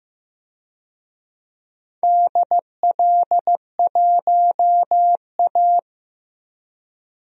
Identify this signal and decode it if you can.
Morse code, heard as a keyed tone, about 15 wpm, 700 Hz. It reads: DL1A
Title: DXCC-100 Call Signs - 4 Characters - 15wpm